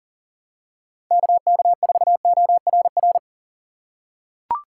Morse code, heard as a keyed tone, about 40 wpm, 700 Hz. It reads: KK4ORR
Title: USA Call Signs - 2x Character Spacing - 40wpm